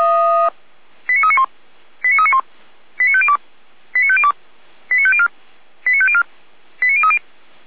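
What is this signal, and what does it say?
Five-tone selective call (ZVEI), heard as quick runs of five tones, tones of about 70 ms. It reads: 89391 89391 89502 89502 89504 89504 89030
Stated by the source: ZVEI five-tone selective call, received by SDR.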